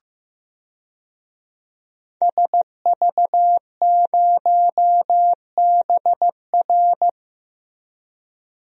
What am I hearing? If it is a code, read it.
Morse code, heard as a keyed tone, about 15 wpm, 700 Hz. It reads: SV0BR
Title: DXCC-100 Call Signs - 5 Characters - Repeated 3x - 15wpm